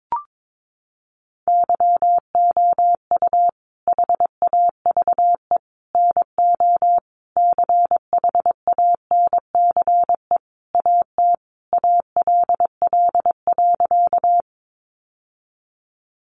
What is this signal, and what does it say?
Morse code, heard as a keyed tone, about 22 wpm, 700 Hz. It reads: YOU HAVE NO CHANCE AT ALL.
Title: Sentences from Top 800 Words 22wpm